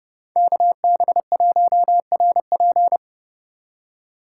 Morse code, heard as a keyed tone, about 30 wpm, 700 Hz. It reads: KB1RP